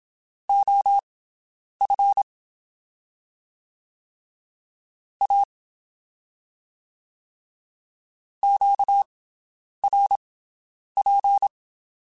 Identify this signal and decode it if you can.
Morse code, heard as a keyed tone, about 27 wpm, spaced as for 9 wpm, 800 Hz. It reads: OF A QRP
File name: (spaced) / QSOs Spaced-mp3s / WD0BGP 1 @25 sp